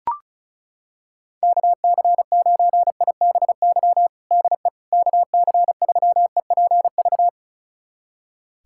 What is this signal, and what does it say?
Morse code, heard as a keyed tone, about 34 wpm, 700 Hz. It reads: KC9IBY DE KC3EPV
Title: USA Callsign DE Callsign - 6 Characters - 35wpm